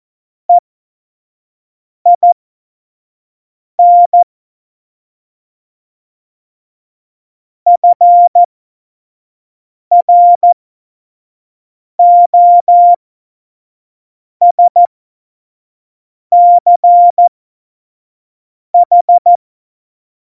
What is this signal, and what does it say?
Morse code, heard as a keyed tone, about 14 wpm, spaced as for 5 wpm, 700 Hz. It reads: EIN FROSCH
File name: Morsekurs / Witze / Witz 10 April 2018, 25 F70